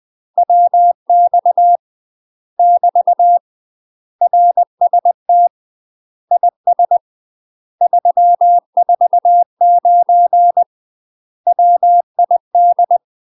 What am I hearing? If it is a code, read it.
Morse code, heard as a keyed tone, about 20 wpm, 700 Hz. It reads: WX <BT> RST IS 349 WID
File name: UK MASTER QSOs TRAINING / 20WPM / UK MASTER QSOs 0051